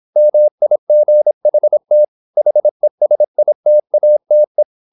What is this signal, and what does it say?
Morse code, heard as a keyed tone, about 26 wpm, 600 Hz. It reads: MIGHT HESITATE
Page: Never 26wpm | CW med Gnister